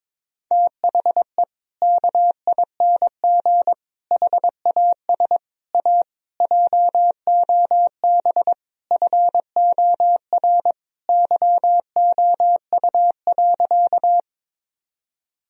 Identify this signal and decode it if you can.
Morse code, heard as a keyed tone, about 22 wpm, 700 Hz. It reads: THE KING HAS A JOB FOR YOU.
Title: Sentences from Top 1000 Words - Review Part 3 - 22wpm